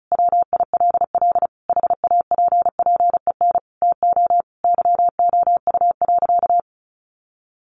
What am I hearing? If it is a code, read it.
Morse code, heard as a keyed tone, about 34 wpm, 700 Hz. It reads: WILL HAPPEN TO YOU.
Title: Sentences from Top 500 Words 35wpm